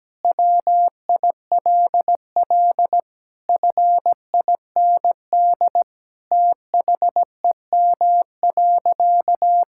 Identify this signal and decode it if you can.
Morse code, heard as a keyed tone, about 17 wpm, 700 Hz. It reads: WILL FIND THEM.
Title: Sentences from Top 1000 Words - Review Part 6 - 17wpm